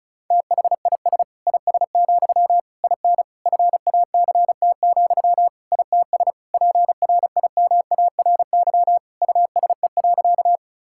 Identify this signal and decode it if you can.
Morse code, heard as a keyed tone, about 36 wpm, 700 Hz. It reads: THIS IS, IN FACT, ITS PRIMARY USE.
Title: Sentences from Top 1000 Words - Review Part 5 - 35wpm